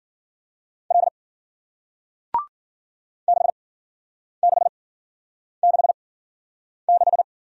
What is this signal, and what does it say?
Morse code, heard as a keyed tone, about 48 wpm, spaced as for 9 wpm, 700 Hz. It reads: L 6666
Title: Speed-Racing Single Letter-Number 40wpm